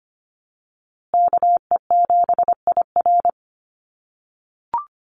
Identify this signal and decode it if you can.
Morse code, heard as a keyed tone, about 25 wpm, 700 Hz. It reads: KE7IR
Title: USA Call Signs Repeated 2x 25wpm